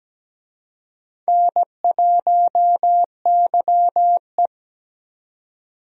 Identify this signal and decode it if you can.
Morse code, heard as a keyed tone, about 17 wpm, 700 Hz. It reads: N1YE
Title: USA Call Signs - 4 Characters - 17wpm